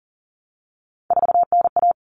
Morse code, heard as a keyed tone, about 40 wpm, 700 Hz. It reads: 4NA
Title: Common Cut Numbers - 3-Digit 40wpm